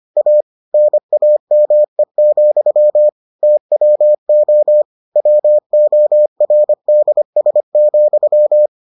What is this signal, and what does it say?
Morse code, heard as a keyed tone, about 25 wpm, 600 Hz. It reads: A NAME, TWO WORDS,